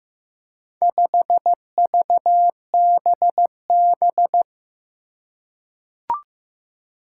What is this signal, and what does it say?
Morse code, heard as a keyed tone, about 15 wpm, 700 Hz. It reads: 5VBB
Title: Four Challenging Letter-Number Combinations 15wpm